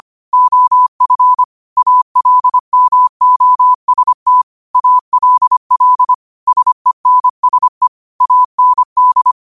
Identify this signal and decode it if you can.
Morse code, heard as a keyed tone, about 25 wpm, 1000 Hz. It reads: OF ALMOST ALL SENSE AND